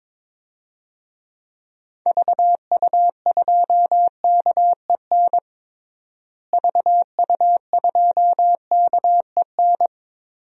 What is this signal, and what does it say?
Morse code, heard as a keyed tone, about 22 wpm, 700 Hz. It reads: VU2KEN VU2KEN
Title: DXCC-100 Call Signs - 6 Characters - Repeated 3x - 22wpm